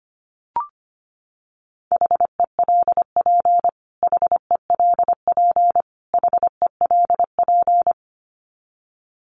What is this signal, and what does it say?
Morse code, heard as a keyed tone, about 25 wpm, 700 Hz. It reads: HELP HELP HELP
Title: Top 500 Words Repeated 25wpm